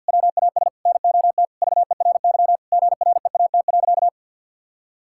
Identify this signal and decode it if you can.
Morse code, heard as a keyed tone, about 50 wpm, 700 Hz. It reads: WAS NOT VERY GREAT.
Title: Sentences from Top 1000 Words - Review Part 5 - 50wpm